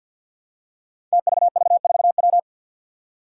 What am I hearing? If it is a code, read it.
Morse code, heard as a keyed tone, about 50 wpm, 700 Hz. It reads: TVV4W
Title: Five Challenging Letter-Number Combinations 50wpm